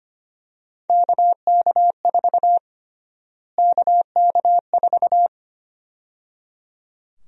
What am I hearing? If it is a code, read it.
Morse code, heard as a keyed tone, about 25 wpm, 700 Hz. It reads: KK4 KK4
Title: USA Call Sign Prefixes - Repeated 2x - 25wpm